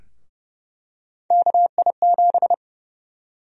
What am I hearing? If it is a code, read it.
Morse code, heard as a keyed tone, about 30 wpm, 700 Hz. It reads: KI7
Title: USA Call Sign Prefixes - Repeated 2x - 30wpm